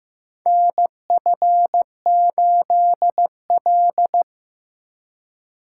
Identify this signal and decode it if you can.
Morse code, heard as a keyed tone, about 15 wpm, 700 Hz. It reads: NF8L
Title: USA Call Signs - 3x Character Spacing - 15wpm